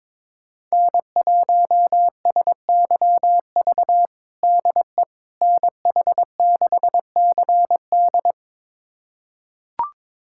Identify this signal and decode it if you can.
Morse code, heard as a keyed tone, about 22 wpm, 700 Hz. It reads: N1SYV DE NH6CD
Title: USA Callsign DE Callsign - 5 Characters - 22wpm